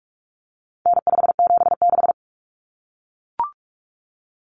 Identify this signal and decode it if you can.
Morse code, heard as a keyed tone, about 44 wpm, 700 Hz. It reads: N576
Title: Common Cut Numbers - 4-Digit 45wpm